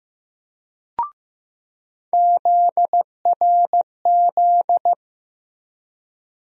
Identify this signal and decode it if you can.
Morse code, heard as a keyed tone, about 15 wpm, 700 Hz. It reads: ZRZ
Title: USA Call Sign Postfixes - 3 Letters 15wpm